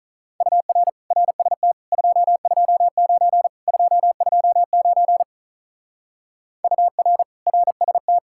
Morse code, heard as a keyed tone, about 43 wpm, 700 Hz. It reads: UR RST 229 229 UR RST
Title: Speed-Racing Common QSO Phrases - 35wpm